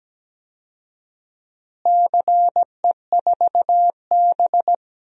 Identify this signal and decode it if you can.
Morse code, heard as a keyed tone, about 17 wpm, 700 Hz. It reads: CE4B
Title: DXCC-100 Call Signs - 4 Characters - Repeated 2x - 17wpm